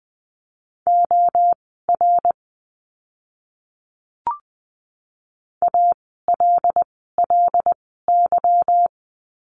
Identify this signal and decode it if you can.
Morse code, heard as a keyed tone, about 20 wpm, spaced as for 15 wpm, 700 Hz. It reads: OR ALLY
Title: 146 Lesson 24f - Letter y - Words 20wpm